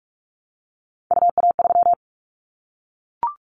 Morse code, heard as a keyed tone, about 44 wpm, 700 Hz. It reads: UA3
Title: Callsign Prefixes for Top 100 DXCC - 45wpm